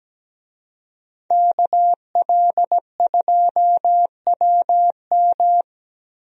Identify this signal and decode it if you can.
Morse code, heard as a keyed tone, about 17 wpm, 700 Hz. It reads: KL2WM
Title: USA Call Signs - 5 Characters - Repeated 3x - 17wpm